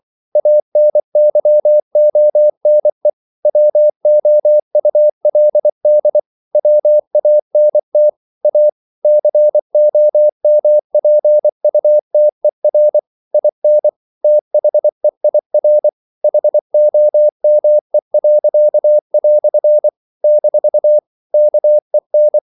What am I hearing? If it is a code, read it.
Morse code, heard as a keyed tone, about 24 wpm, 600 Hz. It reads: ANYONE WOULD WANT A COMPUTER IN THEIR HOME." - KEN
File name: CW / Practice / EnglishQuotations0424